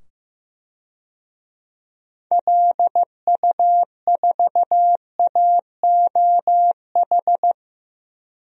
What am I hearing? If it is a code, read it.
Morse code, heard as a keyed tone, about 15 wpm, 700 Hz. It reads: LU4AOH
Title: DXCC-100 Call Signs - 6 Characters - 15wpm